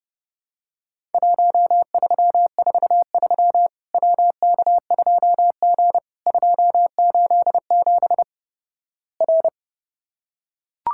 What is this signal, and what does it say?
Morse code, heard as a keyed tone, about 32 wpm, 700 Hz. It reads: 1343 WK2G 287 R
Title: Worked All Europe DX Contest - QTC Exchange - 30wpm